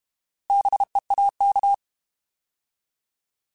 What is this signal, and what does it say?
Morse code, heard as a keyed tone, about 32 wpm, 800 Hz. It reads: BEAK
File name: Alphabetical / Alphabetical mp3s / Bs @30